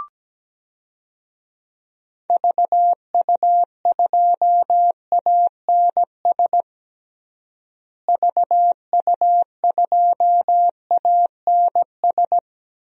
Morse code, heard as a keyed tone, about 17 wpm, 700 Hz. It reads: VU2ANS VU2ANS
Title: DXCC-100 Call Signs - 6 Characters - Repeated 3x - 17wpm